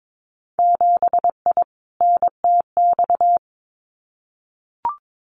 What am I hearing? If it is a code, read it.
Morse code, heard as a keyed tone, about 22 wpm, 700 Hz. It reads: 7I NTX
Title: Winter Field Day Exchanges Encore 22wpm